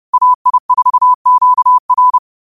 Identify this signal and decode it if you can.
Morse code, heard as a keyed tone, about 30 wpm, 1000 Hz. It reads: AI4QR